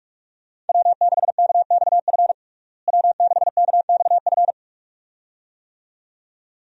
Morse code, heard as a keyed tone, about 46 wpm, 700 Hz. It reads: W6KXF W6KXF
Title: USA Call Signs - 5 Characters - Repeated 2x - 45wpm